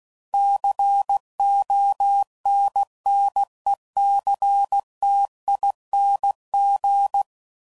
Morse code, heard as a keyed tone, about 16 wpm, 800 Hz. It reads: CONNECTING
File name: Suffixes / Special INGs / Sngl Ings